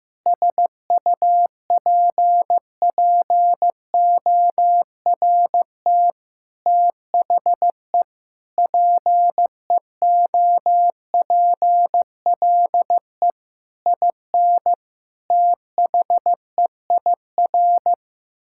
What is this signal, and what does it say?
Morse code, heard as a keyed tone, about 15 wpm, 700 Hz. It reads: SUPPORT THE PEOPLE IN THEIR
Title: Sentences from Top 1000 Words - Review Part 5 - 15wpm